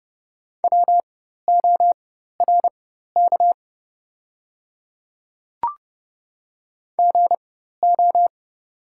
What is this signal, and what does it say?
Morse code, heard as a keyed tone, about 30 wpm, spaced as for 14 wpm, 700 Hz. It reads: WORK GO
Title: 4x Character Spacing - Top 200 Words - 30wpm